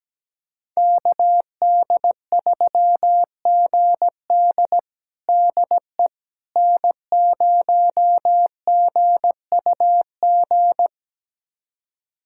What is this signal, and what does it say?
Morse code, heard as a keyed tone, about 17 wpm, 700 Hz. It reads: KD3GD DE N0GUG
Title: USA Callsign DE Callsign - 5 Characters - 17wpm